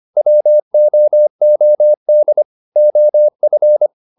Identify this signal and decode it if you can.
Morse code, heard as a keyed tone, about 25 wpm, 600 Hz. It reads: WOOD OF
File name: the old man and the sea part 1 25wpm